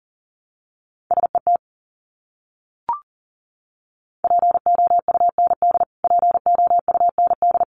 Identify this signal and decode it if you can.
Morse code, heard as a keyed tone, about 40 wpm, 700 Hz. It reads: SET POUND POUND
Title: Top 500 Words Repeated 40wpm